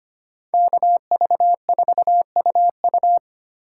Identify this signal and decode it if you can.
Morse code, heard as a keyed tone, about 32 wpm, 700 Hz. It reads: KV4UU